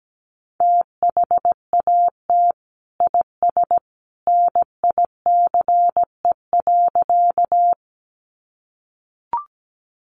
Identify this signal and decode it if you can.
Morse code, heard as a keyed tone, about 17 wpm, 700 Hz. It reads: THAT IS NICE.
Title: Sentences from Top 1100 Words 17wpm